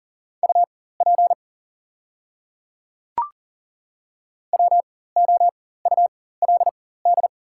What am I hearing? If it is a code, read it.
Morse code, heard as a keyed tone, about 40 wpm, spaced as for 19 wpm, 700 Hz. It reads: UP WOULD
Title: 4x Character Spacing - Top 200 Words - 40wpm